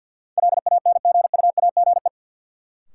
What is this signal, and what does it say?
Morse code, heard as a keyed tone, about 48 wpm, 700 Hz. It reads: LANGUAGE
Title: Sentences from Top 500 Words - New Word Drill - 50wpm